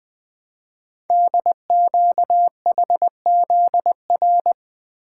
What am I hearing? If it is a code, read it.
Morse code, heard as a keyed tone, about 20 wpm, 700 Hz. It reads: DQHZR
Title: Code Groups - 5 Letter:Numbers - 20wpm